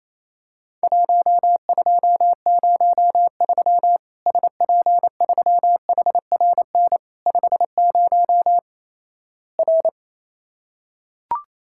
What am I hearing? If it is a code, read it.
Morse code, heard as a keyed tone, about 28 wpm, 700 Hz. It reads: 1203 SP3HRN 50 R
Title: Worked All Europe DX Contest - QTC Exchange - 28wpm